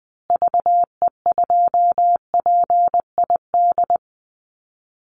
Morse code, heard as a keyed tone, about 19 wpm, 700 Hz. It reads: VE2PID
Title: Call Signs Repeated 3x - 20wpm